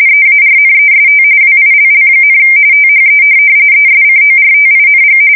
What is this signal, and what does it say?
RTTY, about 45.45 baud. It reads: EARNEST WILL, THE ESCORTING
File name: fsk170 / USS-Missouri